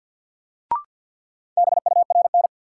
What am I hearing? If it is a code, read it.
Morse code, heard as a keyed tone, about 53 wpm, 700 Hz. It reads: BURN
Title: Sentences from Top 1300 Words - New Word Drill Alt - 50wpm